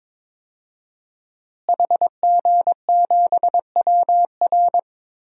Morse code, heard as a keyed tone, about 22 wpm, 700 Hz. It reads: HG7WR
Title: DXCC-100 Call Signs - 5 Characters - Repeated 2x - 22wpm